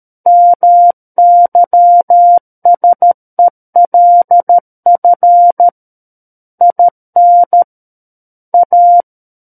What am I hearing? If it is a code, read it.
Morse code, heard as a keyed tone, about 13 wpm, 700 Hz. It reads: MYSELF IN A
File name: Book One / 04-Chapter 4 - 13 WPM